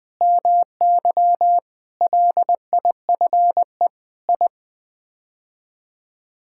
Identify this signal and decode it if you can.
Morse code, heard as a keyed tone, about 20 wpm, 700 Hz. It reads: MY LIFE I
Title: Sets of 8 Words - 20wpm